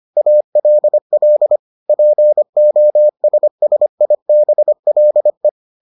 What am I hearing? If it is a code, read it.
Morse code, heard as a keyed tone, about 25 wpm, 600 Hz. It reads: ALL POSSIBLE